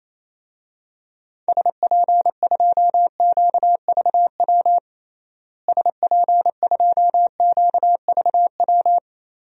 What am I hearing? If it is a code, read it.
Morse code, heard as a keyed tone, about 28 wpm, 700 Hz. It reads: SP2QVW SP2QVW
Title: DXCC-100 Call Signs - 6 Characters - Repeated 3x - 28wpm